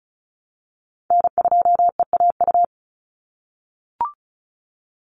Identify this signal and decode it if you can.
Morse code, heard as a keyed tone, about 36 wpm, 700 Hz. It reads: N2EAU
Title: USA Call Signs 35wpm